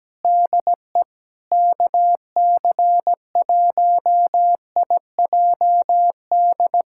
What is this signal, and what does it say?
Morse code, heard as a keyed tone, about 17 wpm, 700 Hz. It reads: DE KC1IJD
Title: USA Callsign DE Callsign - 6 Characters - 17wpm